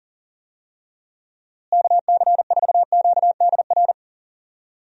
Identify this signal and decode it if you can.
Morse code, heard as a keyed tone, about 39 wpm, 700 Hz. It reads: KC4QDR
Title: USA Call Signs - 6 Characters - Repeated 2x - 40wpm